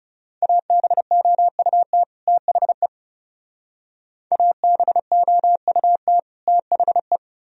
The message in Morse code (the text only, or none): ABOUT THE ABOUT THE